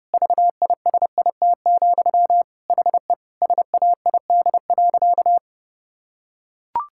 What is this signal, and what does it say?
Morse code, heard as a keyed tone, about 32 wpm, 700 Hz. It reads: VISIT, HE SAID.
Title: Sentences from Top 1000 Words - Review Part 6 - 30wpm